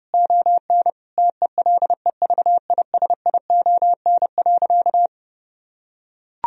Morse code, heard as a keyed tone, about 32 wpm, 700 Hz. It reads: ON TELEVISION.
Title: Sentences from Top 1000 Words - Review Part 4 - 30wpm